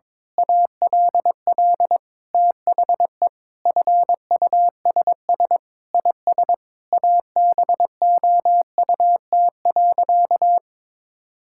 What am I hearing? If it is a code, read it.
Morse code, heard as a keyed tone, about 22 wpm, 700 Hz. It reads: ALL THE FUSS IS ABOUT.
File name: practice / 22wpm / Sentences with 8-Word Phrases - 22wpm